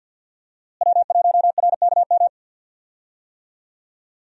Morse code, heard as a keyed tone, about 45 wpm, 700 Hz. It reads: W1RKM
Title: USA Call Signs - 5 Characters - Repeated 2x - 50wpm